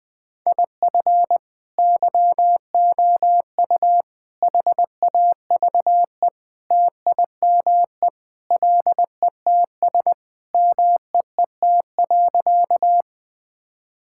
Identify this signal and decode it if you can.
Morse code, heard as a keyed tone, about 20 wpm, 700 Hz. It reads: IF YOU HAVE TIME LETS MEET.
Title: Sentences with 3-Word Phrases - 20wpm